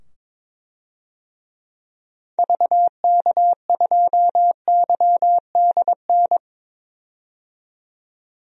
Morse code, heard as a keyed tone, about 22 wpm, 700 Hz. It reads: VK2YDN
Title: DXCC-100 Call Signs - 6 Characters - 22wpm